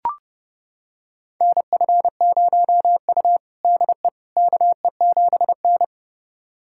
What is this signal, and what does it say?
Morse code, heard as a keyed tone, about 30 wpm, 700 Hz. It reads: NF0U DE KE7N
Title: USA Callsign DE Callsign - 4 Characters - 30wpm